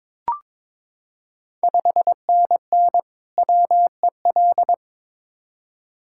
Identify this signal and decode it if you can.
Morse code, heard as a keyed tone, about 22 wpm, 700 Hz. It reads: 5NN WEL
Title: Ontario QSO Party Exchange 22wpm